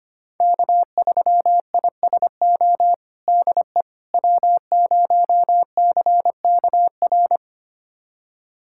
Morse code, heard as a keyed tone, about 25 wpm, 700 Hz. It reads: K3ISO DE W0CKR
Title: USA Callsign DE Callsign - 5 Characters - 25wpm